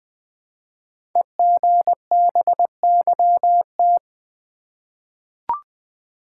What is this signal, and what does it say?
Morse code, heard as a keyed tone, about 20 wpm, 700 Hz. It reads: EGBYT
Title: Code Groups - 5 Letter:Numbers - 20wpm